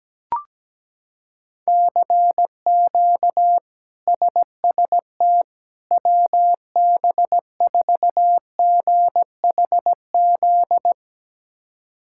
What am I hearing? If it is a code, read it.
Morse code, heard as a keyed tone, about 17 wpm, 700 Hz. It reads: CQ SST WB4GHZ
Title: K1USN SST QSO Practice - 17wpm